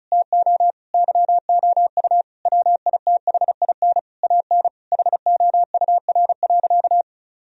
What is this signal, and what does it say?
Morse code, heard as a keyed tone, about 36 wpm, 700 Hz. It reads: TO YOU WITHIN AN HOUR.
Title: Sentences from Top 1000 Words - Review Part 4 - 35wpm